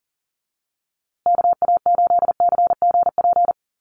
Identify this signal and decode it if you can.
Morse code, heard as a keyed tone, about 40 wpm, 700 Hz. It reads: KA8CGP
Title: USA Call Signs Repeated 2x 40wpm